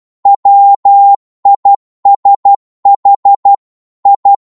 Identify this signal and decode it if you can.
Morse code, heard as a keyed tone, about 12 wpm, 800 Hz. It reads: WISH I